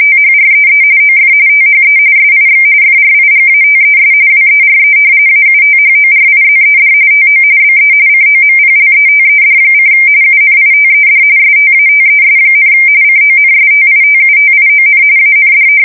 RTTY, about 45.45 baud. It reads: THREE OF THE SIX FRIGATES WERE AUTHORIZED TO BE COMPLETED: UNITED STATES, CONSTITUTION AND
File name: fsk170 / usnavy1